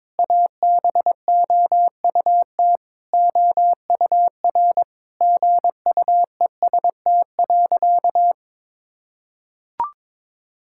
Morse code, heard as a keyed tone, about 22 wpm, 700 Hz. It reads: ABOUT OUR GUEST.
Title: Sentences from Top 1600 Words 22wpm